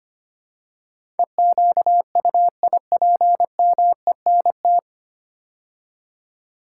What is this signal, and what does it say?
Morse code, heard as a keyed tone, about 25 wpm, 700 Hz. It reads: EQUIPMENT
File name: practice / 25wpm / 9 Letter Words - First 3 Letters Given - 25wpm